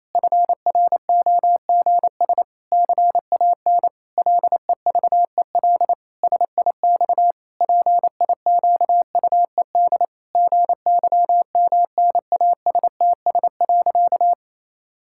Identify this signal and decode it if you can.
Morse code, heard as a keyed tone, about 28 wpm, 700 Hz. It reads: FROGS CAN LEVEL SIX PIQUED GYMNASTS.
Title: Pangram Challenge v2 - 28wpm